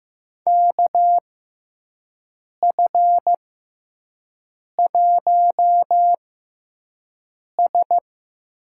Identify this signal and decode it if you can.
Morse code, heard as a keyed tone, about 15 wpm, spaced as for 5 wpm, 700 Hz. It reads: KF1S
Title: USA Call Signs - 4 Characters - 6x Character Spacing - 15wpm